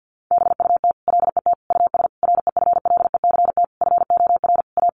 Morse code, heard as a keyed tone, about 50 wpm, 700 Hz. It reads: BUT LET US REFLECT FOR A
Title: Sentences from Top 1500 Words 50wpm